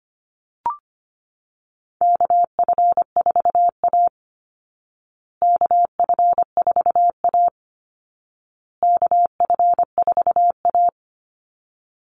Morse code, heard as a keyed tone, about 25 wpm, 700 Hz. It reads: KF4A KF4A KF4A
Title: Call Signs Repeated 2x - 25wpm